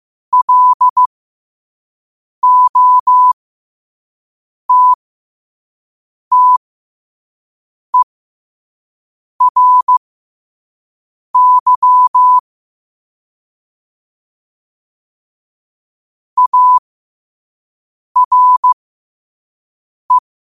Morse code, heard as a keyed tone, about 15 wpm, spaced as for 5 wpm, 1000 Hz. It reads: LOTTERY ARE
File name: qotdfiles / 5wpm